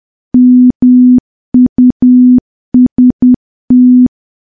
Morse code, heard as a keyed tone, about 10 wpm, 250 Hz. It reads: MUST
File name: morse / morse-training-10WPM